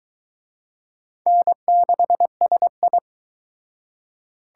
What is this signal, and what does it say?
Morse code, heard as a keyed tone, about 22 wpm, 700 Hz. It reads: N6SI